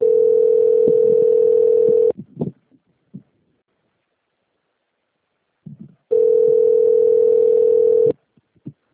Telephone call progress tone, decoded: ringback tone

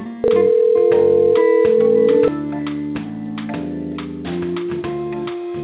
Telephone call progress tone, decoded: ringback tone

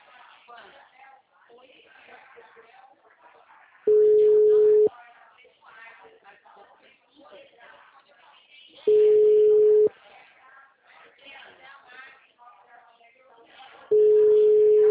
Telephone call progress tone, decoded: ringback tone

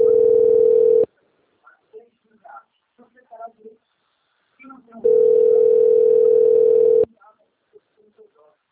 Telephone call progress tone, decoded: ringback tone

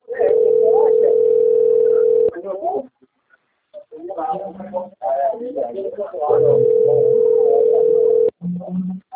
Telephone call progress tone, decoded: ringback tone

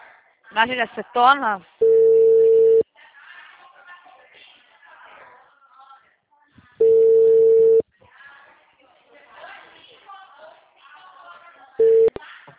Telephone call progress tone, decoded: ringback tone

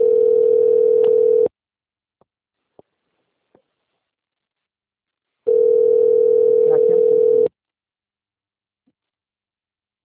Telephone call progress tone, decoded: ringback tone

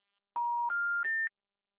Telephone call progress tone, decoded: special information tone

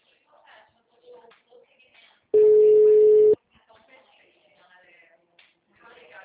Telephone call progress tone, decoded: ringback tone

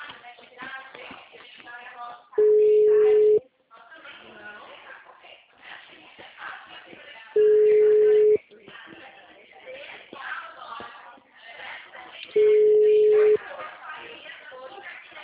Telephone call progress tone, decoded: ringback tone

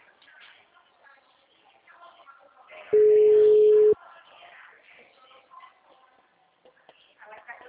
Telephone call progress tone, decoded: ringback tone

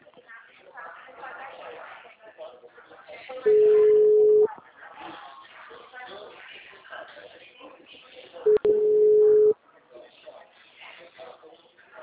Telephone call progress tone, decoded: ringback tone

